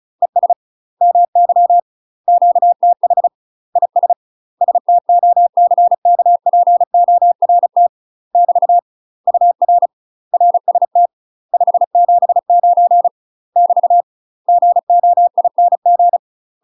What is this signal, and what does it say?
Morse code, heard as a keyed tone, about 34 wpm, 700 Hz. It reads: ES MY QTH IS STOCKPORT <BT> UR RST 579 <BT> GOING
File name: UK MASTER QSOs TRAINING/35WPM